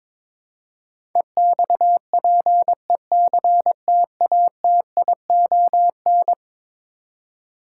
Morse code, heard as a keyed tone, about 21 wpm, 700 Hz. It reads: EXPECTATION
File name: practice / 22wpm / Eleven Letter Words - 22wpm